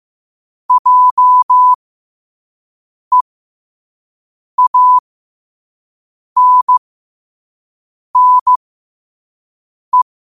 Morse code, heard as a keyed tone, about 15 wpm, spaced as for 5 wpm, 1000 Hz. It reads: JEANNE
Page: Quotes for Sat, 16 Aug 2025 in Morse Code at 5 words per minute.